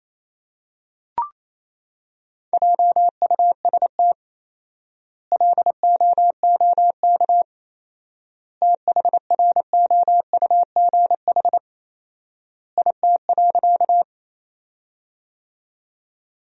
Morse code, heard as a keyed tone, about 28 wpm, 700 Hz. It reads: JUST LOOK THROUGH IT.
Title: 4x Word-Spacing - Sentences from Top 200 Words - 28wpm